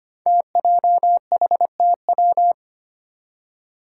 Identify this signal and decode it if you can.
Morse code, heard as a keyed tone, about 25 wpm, 700 Hz. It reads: TJHTW
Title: Five Challenging Letter-Number Combinations 25wpm